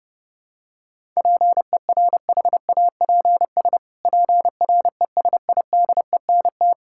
Morse code, heard as a keyed tone, about 30 wpm, 700 Hz. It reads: PERHAPS PRESIDENT
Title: Pangram Challenge v2 - 30wpm